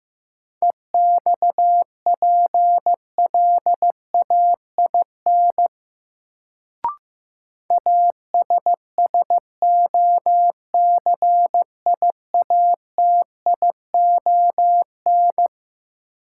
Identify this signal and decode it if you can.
Morse code, heard as a keyed tone, about 15 wpm, 700 Hz. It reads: EXPLAIN ASSOCIATION
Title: Sentences from Top 1100 Words - New Word Drill Alt - 15wpm